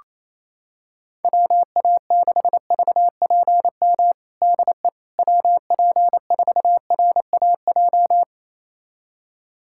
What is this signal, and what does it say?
Morse code, heard as a keyed tone, about 28 wpm, 700 Hz. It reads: WA6VPM DE WP4RAJ
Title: USA Callsign DE Callsign - 6 Characters - 28wpm